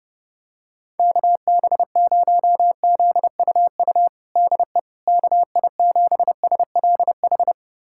Morse code, heard as a keyed tone, about 32 wpm, 700 Hz. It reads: KB0ZUU DE KI7SLH
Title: USA Callsign DE Callsign - 6 Characters - 30wpm